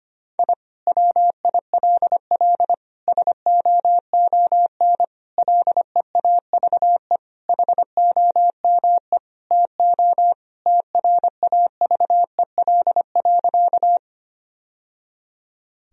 Morse code, heard as a keyed tone, about 25 wpm, 700 Hz. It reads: I WILL SOON LEAVE HOME TO TRAVEL.
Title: Sentences from Top 1000 Words - Review Part 2 - 25wpm